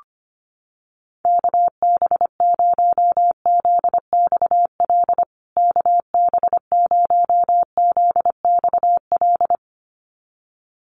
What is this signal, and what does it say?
Morse code, heard as a keyed tone, about 25 wpm, 700 Hz. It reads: KB0ZXL KB0ZXL
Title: USA Call Signs Repeated 2x 25wpm